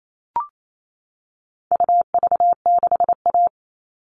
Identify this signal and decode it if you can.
Morse code, heard as a keyed tone, about 28 wpm, 700 Hz. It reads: UV6A
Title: All Cut Numbers - 4-Digit 28wpm